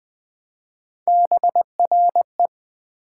Morse code, heard as a keyed tone, about 20 wpm, 700 Hz. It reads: BRE
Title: Build A Word - 1 Letter At A Time - 3 to 5-Letter Words - 20wpm